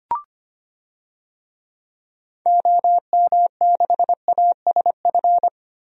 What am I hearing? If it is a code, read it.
Morse code, heard as a keyed tone, about 25 wpm, 700 Hz. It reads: OM6ASF
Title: DXCC-100 Call Signs - 6 Characters - Repeated 3x - 25wpm